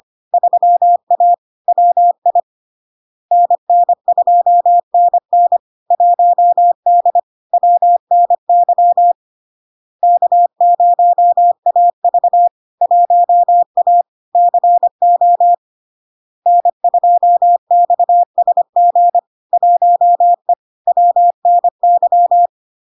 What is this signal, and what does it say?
Morse code, heard as a keyed tone, about 25 wpm, 700 Hz. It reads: 3A WI NN2NN 1D WNY K0AV 1A CO N2XSG 1E WNY
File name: LARK-FD-2022-Log-in-CW-25wpm-700-Hz